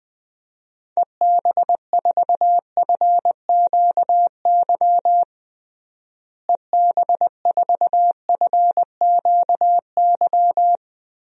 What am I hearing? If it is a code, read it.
Morse code, heard as a keyed tone, about 20 wpm, 700 Hz. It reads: EB4FQY EB4FQY
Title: DXCC-100 Call Signs - 6 Characters - Repeated 3x - 20wpm